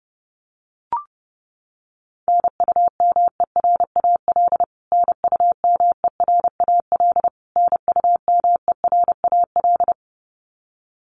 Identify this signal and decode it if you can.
Morse code, heard as a keyed tone, about 28 wpm, 700 Hz. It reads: NUMERAL NUMERAL NUMERAL
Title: Top 500 Words Repeated 30wpm